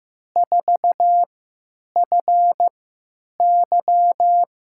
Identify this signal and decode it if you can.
Morse code, heard as a keyed tone, about 15 wpm, spaced as for 9 wpm, 700 Hz. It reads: 4FY
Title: USA Call Signs - 3x Character Spacing - 15wpm